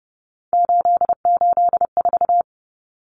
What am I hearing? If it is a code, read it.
Morse code, heard as a keyed tone, about 30 wpm, 700 Hz. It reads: 884
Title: Common Cut Numbers - 3-Digit 30wpm